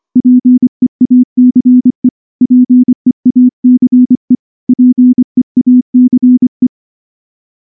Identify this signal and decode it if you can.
Morse code, heard as a keyed tone, about 24 wpm, 250 Hz. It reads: PEACE PEACE PEACE
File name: morse / morse-training-25WPM